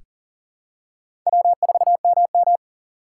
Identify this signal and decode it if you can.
Morse code, heard as a keyed tone, about 40 wpm, 700 Hz. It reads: W4MM
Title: USA Call Signs - 4 Characters - 2x Character Spacing - 40wpm